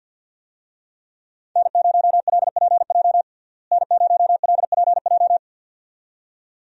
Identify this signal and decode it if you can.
Morse code, heard as a keyed tone, about 48 wpm, 700 Hz. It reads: N0LPJ N0LPJ
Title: USA Call Signs - 5 Characters - Repeated 2x - 50wpm